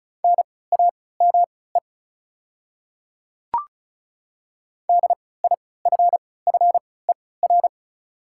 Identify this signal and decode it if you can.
Morse code, heard as a keyed tone, about 34 wpm, spaced as for 20 wpm, 690 Hz. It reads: NAME DIFFER
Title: 3x Character Spacing - Top 300 Words - 35wpm